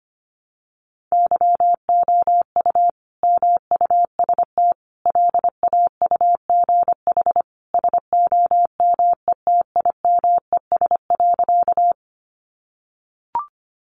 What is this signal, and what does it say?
Morse code, heard as a keyed tone, about 25 wpm, 700 Hz. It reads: YOU MUST LAUGH SOMETIMES.
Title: Sentences from Top 600 Words 25wpm